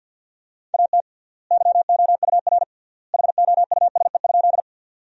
Morse code, heard as a keyed tone, about 50 wpm, 700 Hz. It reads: AT YOUR HOUSE?
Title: 3x Word-Spacing - Sentences from Top 200 Words - 50wpm